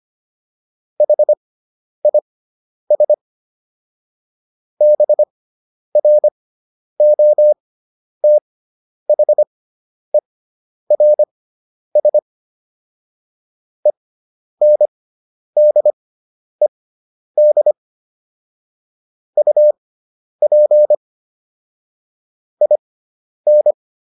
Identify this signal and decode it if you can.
Morse code, heard as a keyed tone, about 25 wpm, spaced as for 10 wpm, 600 Hz. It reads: HIS BROTHERS ENDED UP IN